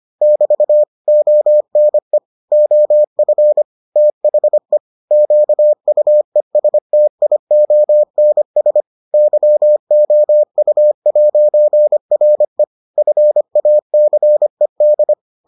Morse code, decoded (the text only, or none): <BT> ONE OF THE QUESTIONS YOU'RE FACED